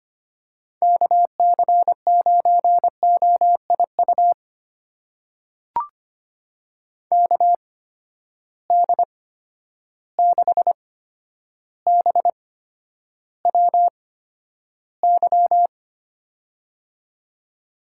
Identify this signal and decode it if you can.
Morse code, heard as a keyed tone, about 25 wpm, spaced as for 11 wpm, 700 Hz. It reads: KC9OIU KD6BWY
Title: USA Call Signs - 6 Characters - 8x Character Spacing - 25wpm